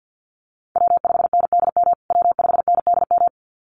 Morse code, heard as a keyed tone, about 50 wpm, 700 Hz. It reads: W5NDM W5NDM
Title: USA Call Signs Repeated 2x 50wpm